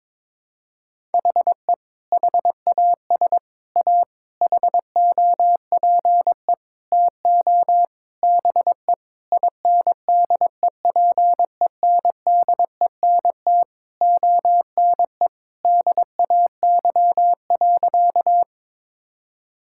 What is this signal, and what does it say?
Morse code, heard as a keyed tone, about 22 wpm, 700 Hz. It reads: HE HAS A HOPE TO BE INDEPENDENT ONE DAY.
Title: Sentences from Top 1000 Words - Review Part 6 - 22wpm